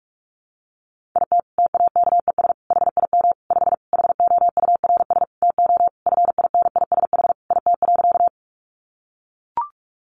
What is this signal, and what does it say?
Morse code, heard as a keyed tone, about 44 wpm, 700 Hz. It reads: IT TAKES HIM 5 HOURS TO FINISH IT.
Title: Additional Easy Phrases 45wpm